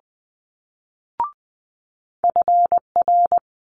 Morse code, heard as a keyed tone, about 20 wpm, 700 Hz. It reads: FR
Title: Multi-letter Phonemes Quick Fire 20wpm